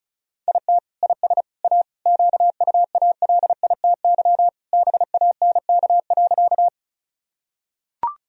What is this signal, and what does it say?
Morse code, heard as a keyed tone, about 36 wpm, 700 Hz. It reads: IT IS A QUALITY BANK.
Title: Sentences from Top 1000 Words - Review Part 4 - 35wpm